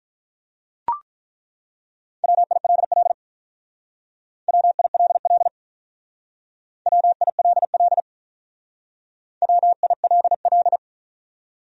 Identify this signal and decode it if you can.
Morse code, heard as a keyed tone, about 48 wpm, 700 Hz. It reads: WILL WILL WILL WILL